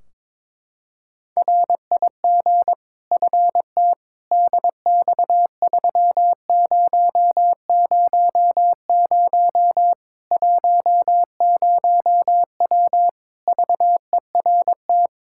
Morse code, heard as a keyed tone, about 22 wpm, 700 Hz. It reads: RIG FT DX3000 10W VERT
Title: Practice QSO - Exchange 2 - Info - 22wpm